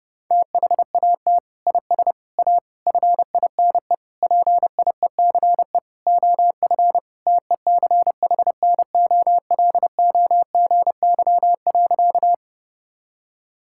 Morse code, heard as a keyed tone, about 32 wpm, 700 Hz. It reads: THAT IS A FINE PIECE OF TECHNOLOGY.
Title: Sentences from Top 1000 Words - Review Part 5 - 30wpm